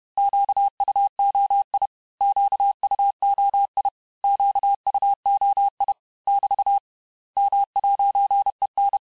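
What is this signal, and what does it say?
Morse code, heard as a keyed tone, about 31 wpm, 800 Hz. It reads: QUOI QUOI QUOI <BT> M'EN
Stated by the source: Morse lesson of words, each word repeated 4 times, at 29 words per minute.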